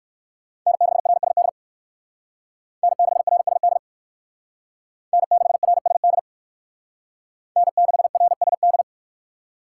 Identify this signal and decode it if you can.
Morse code, heard as a keyed tone, about 59 wpm, 700 Hz. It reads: N6RSD N6RSD N6RSD N6RSD